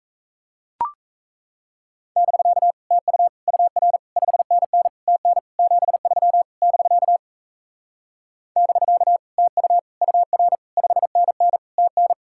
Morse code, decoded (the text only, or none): <BK> TU UR 5NN TN 73 <BK> <BK> TU UR 5NN TN